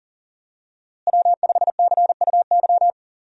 Morse code, heard as a keyed tone, about 40 wpm, 700 Hz. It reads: W5CUY
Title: USA Call Signs - 5 Characters - Repeated 2x - 40wpm